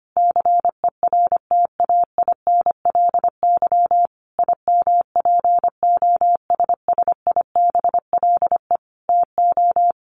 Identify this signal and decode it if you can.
Morse code, heard as a keyed tone, about 25 wpm, 700 Hz. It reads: CERTAINLY IMPOSSIBLE TO
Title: Sentences from Top 1100 Words 25wpm